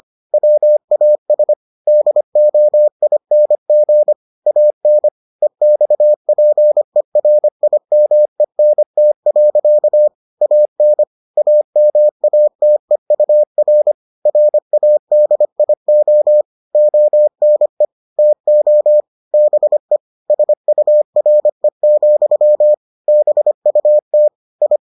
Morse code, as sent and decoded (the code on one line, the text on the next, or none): .-- .- ... / -.. --- .. -. --. / .- -. / . -..- .--. .-. .. -- . -. - .-.-.- / .- -. / .- -- .- - . ..- .-. / .-. .- -.. .. --- / --- -. / - --- / -... / ... ..- .-. --..-- / -... ..- - / ..
WAS DOING AN EXPERIMENT. AN AMATEUR RADIO ONE TO BE SURE, BUT I